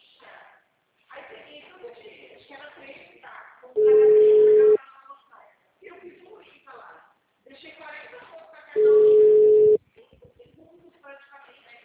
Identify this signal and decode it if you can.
Ringback tone (Europe, 425 Hz), with speech.
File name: out- -2011- - - .7747